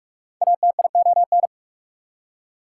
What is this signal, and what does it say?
Morse code, heard as a keyed tone, about 60 wpm, spaced as for 52 wpm, 700 Hz. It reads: ATION
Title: Speed-Racing Prefixes and Suffixes 45wpm